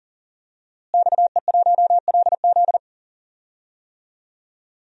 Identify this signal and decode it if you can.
Morse code, heard as a keyed tone, about 40 wpm, 700 Hz. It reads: XE1LZ
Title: DXCC-100 Call Signs - 5 Characters - Repeated 2x - 40wpm